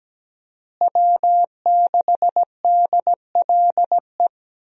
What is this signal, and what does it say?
Morse code, heard as a keyed tone, about 17 wpm, 700 Hz. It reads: W6DLE
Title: USA Call Signs - 5 Characters - Repeated 3x - 17wpm